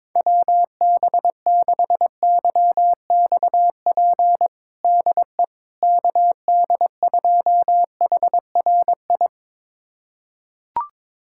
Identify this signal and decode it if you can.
Morse code, heard as a keyed tone, about 22 wpm, 700 Hz. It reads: WB6YXP DE KD2HRI
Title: USA Callsign DE Callsign 22wpm